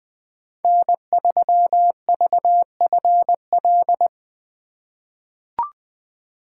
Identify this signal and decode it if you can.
Morse code, heard as a keyed tone, about 20 wpm, 700 Hz. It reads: N3VFL
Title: USA Call Signs - 5 Characters - 3x Character Spacing - 20wpm